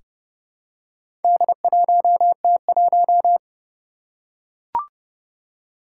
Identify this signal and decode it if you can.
Morse code, heard as a keyed tone, about 30 wpm, 700 Hz. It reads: D1T1
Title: Four Challenging Letter-Number Combinations 30wpm